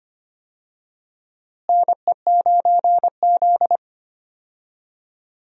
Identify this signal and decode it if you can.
Morse code, heard as a keyed tone, about 25 wpm, 700 Hz. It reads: NE9Z
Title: USA Call Signs - 4 Characters - Repeated 2x - 25wpm